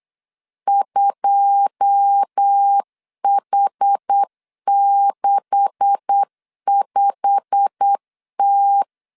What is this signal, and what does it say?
Morse code, heard as a keyed tone, about 8 wpm, 800 Hz. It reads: 2H65T
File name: LEZIONI / 9 286 8wpm 3425